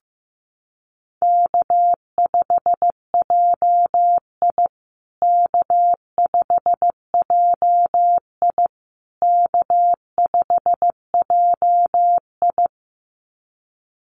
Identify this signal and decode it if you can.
Morse code, heard as a keyed tone, about 15 wpm, 700 Hz. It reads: K5JI K5JI K5JI
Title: USA Call Signs Repeated 3x 15wpm